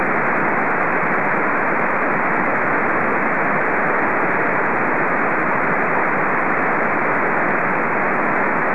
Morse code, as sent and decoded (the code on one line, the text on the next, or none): none